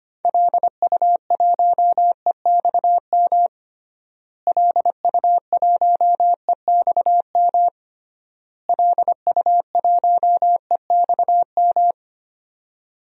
Morse code, heard as a keyed tone, about 25 wpm, 700 Hz. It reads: LU1EXM LU1EXM LU1EXM
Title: DXCC-100 Call Signs - 6 Characters - Repeated 3x - 25wpm